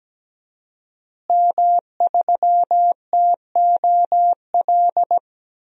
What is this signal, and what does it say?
Morse code, heard as a keyed tone, about 17 wpm, 700 Hz. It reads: M3TOL
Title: DXCC-100 Call Signs - 5 Characters - Repeated 3x - 17wpm